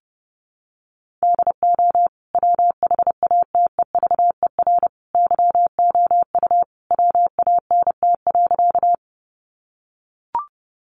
Morse code, heard as a keyed tone, about 30 wpm, 700 Hz. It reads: DO WHATEVER YOU WANT.
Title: Sentences from Top 1100 Words 30wpm